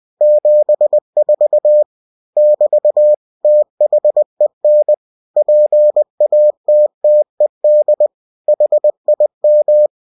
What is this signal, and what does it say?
Morse code, heard as a keyed tone, about 20 wpm, 600 Hz. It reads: 74 <BT> THEN PATTED HIM
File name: cw-books / winnie-the-pooh / winnie-20wpm-page-0074